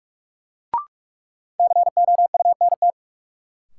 Morse code, heard as a keyed tone, about 46 wpm, 700 Hz. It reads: COUNT
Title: Sentences from Top 1500 Words - New Word Drill - 45wpm